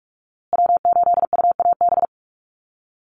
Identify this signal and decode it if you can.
Morse code, heard as a keyed tone, about 44 wpm, 700 Hz. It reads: W8UAB
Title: USA Call Signs Repeated 2x 45wpm